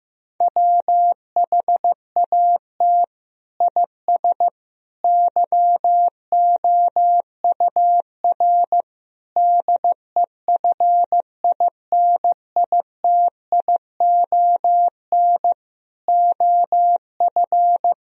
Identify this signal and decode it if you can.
Morse code, heard as a keyed tone, about 15 wpm, 700 Hz. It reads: WHAT IS YOUR DEFINITION OF
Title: Sentences from Top 1000 Words - Review Part 4 - 15wpm